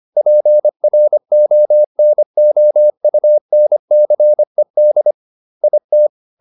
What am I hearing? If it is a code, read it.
Morse code, heard as a keyed tone, about 25 wpm, 600 Hz. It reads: PRONOUNCED IT